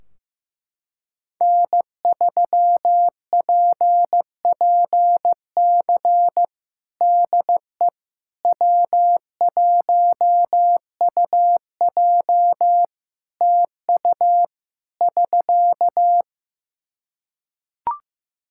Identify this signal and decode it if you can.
Morse code, heard as a keyed tone, about 15 wpm, 700 Hz. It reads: N3PPC DE W1UJ TU <SK>
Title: Practice QSO - Exchange 4 - Ending - 15wpm